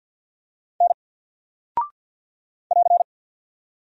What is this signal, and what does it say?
Morse code, heard as a keyed tone, about 48 wpm, 700 Hz. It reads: N <AR>
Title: Single Character (Letter:Number:Punctuation:Prosign) Alt 50wpm